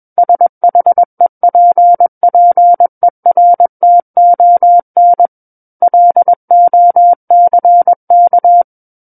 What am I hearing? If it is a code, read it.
Morse code, heard as a keyed tone, about 21 wpm, 700 Hz. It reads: SHEPPERTON LOCK
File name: Book One / 12-Chapter 12 - 21 WPM